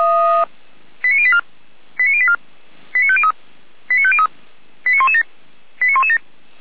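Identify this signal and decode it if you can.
Five-tone selective call (ZVEI), heard as quick runs of five tones, tones of about 70 ms. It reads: 89084 89084 89503 89503 89117 89117
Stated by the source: ZVEI five-tone selective call received on SDR.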